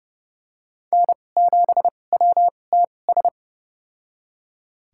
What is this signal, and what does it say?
Morse code, heard as a keyed tone, about 30 wpm, spaced as for 22 wpm, 700 Hz. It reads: N7WTS
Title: USA Call Signs - 5 Characters - 2x Character Spacing - 30wpm